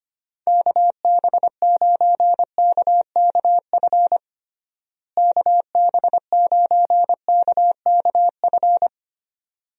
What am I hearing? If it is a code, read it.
Morse code, heard as a keyed tone, about 25 wpm, 700 Hz. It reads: KB9KKF KB9KKF
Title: USA Call Signs - 6 Characters - Repeated 2x - 25wpm